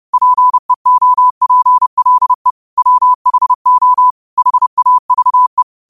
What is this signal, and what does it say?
Morse code, heard as a keyed tone, about 30 wpm, 1000 Hz. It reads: PEOPLE WHO HAVE
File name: qotdfiles / 30wpm